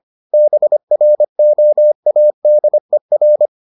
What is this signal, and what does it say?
Morse code, heard as a keyed tone, about 25 wpm, 600 Hz. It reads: BROADER